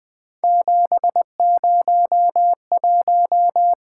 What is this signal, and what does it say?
Morse code, heard as a keyed tone, about 25 wpm, 700 Hz. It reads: 701